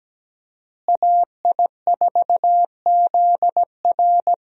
Morse code, heard as a keyed tone, about 17 wpm, 700 Hz. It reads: AI4ZR
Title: USA Call Signs - 5 Characters - Repeated 3x - 17wpm